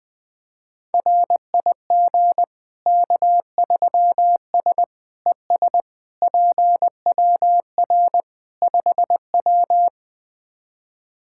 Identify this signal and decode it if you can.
Morse code, heard as a keyed tone, about 20 wpm, 700 Hz. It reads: RIG K3S ES PWR 5W
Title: K1USN SST QSO Practice 20wpm